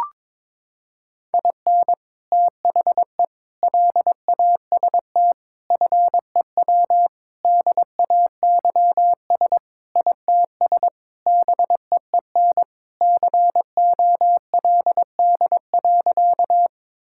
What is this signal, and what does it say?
Morse code, heard as a keyed tone, about 22 wpm, 700 Hz. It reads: IN THE LAST FEW DAYS ITS BEEN COLD.
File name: practice / 22wpm / Sentences with 3-Word Phrases - 22wpm